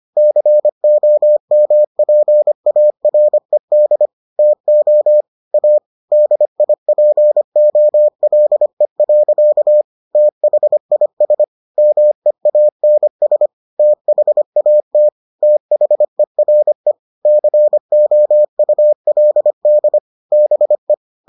Morse code, heard as a keyed tone, about 25 wpm, 600 Hz. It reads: COMPARED TO A DIPOLE. THIS MEANS THAT THERE COULD BE